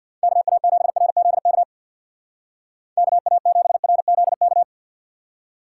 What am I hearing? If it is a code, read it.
Morse code, heard as a keyed tone, about 61 wpm, 700 Hz. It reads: KA7RZK KA7RZK
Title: Speed Racing - USA 2x3 Call Signs 50wpm